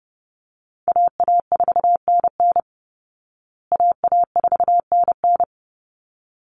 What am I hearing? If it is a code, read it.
Morse code, heard as a keyed tone, about 30 wpm, 700 Hz. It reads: AA4NN AA4NN
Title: Call Signs Repeated 3x - 30wpm